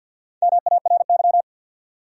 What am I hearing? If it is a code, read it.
Morse code, heard as a keyed tone, about 48 wpm, 700 Hz. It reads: MARY
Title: USA Common Names - Max 5 Letters - 50wpm